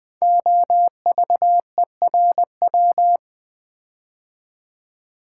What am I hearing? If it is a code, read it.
Morse code, heard as a keyed tone, about 20 wpm, 700 Hz. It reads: OVERW
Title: Build A Word - 5 Letters At A Time - 12-Letter Words - 20wpm